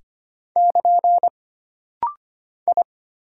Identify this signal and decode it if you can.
Morse code, heard as a keyed tone, about 25 wpm, 700 Hz. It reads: <KN> I
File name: Single Character (Letter:Number:Punctuation:Prosign) - Rapid-Fire - 25wpm